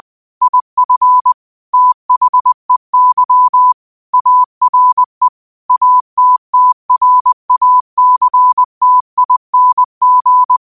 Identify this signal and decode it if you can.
Morse code, heard as a keyed tone, about 20 wpm, 1000 Hz. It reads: IF THEY ARE ATTRACTING
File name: 20wpm